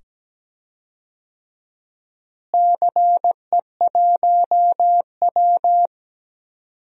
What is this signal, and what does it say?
Morse code, heard as a keyed tone, about 17 wpm, 700 Hz. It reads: CE1W
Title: DXCC-100 Call Signs - 4 Characters - Repeated 2x - 17wpm